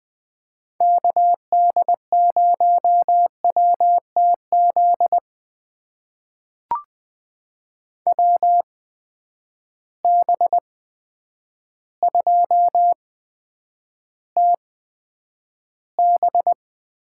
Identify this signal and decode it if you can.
Morse code, heard as a keyed tone, about 21 wpm, spaced as for 9 wpm, 700 Hz. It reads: KD0WTZ WB2TB
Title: USA Call Signs - 6 Characters - 8x Character Spacing - 20wpm